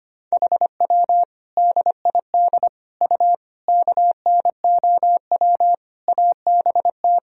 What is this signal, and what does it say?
Morse code, heard as a keyed tone, about 25 wpm, 700 Hz. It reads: HW DID U KNOW ABT
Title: Sentences from Top 100 Words and Abbreviations 25wpm